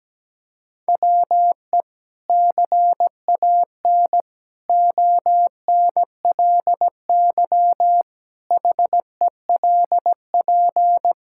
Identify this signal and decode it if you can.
Morse code, heard as a keyed tone, about 17 wpm, 700 Hz. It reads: WE CAN ONLY HELP
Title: Sentences from Top 1000 Words - Review Part 5 - 17wpm